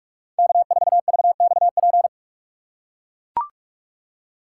Morse code, heard as a keyed tone, about 42 wpm, 700 Hz. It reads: K4VXP
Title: USA Call Signs - 5 Characters - 2x Character Spacing - 45wpm